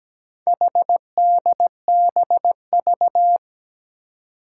Morse code, heard as a keyed tone, about 17 wpm, 700 Hz. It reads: HDBV
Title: Four Challenging Letter-Number Combinations 17wpm